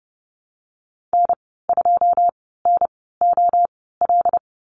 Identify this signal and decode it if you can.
Morse code, heard as a keyed tone, about 30 wpm, spaced as for 17 wpm, 700 Hz. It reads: N2NOL
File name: Lesson 14g - Number 2 - Call Signs 30wpm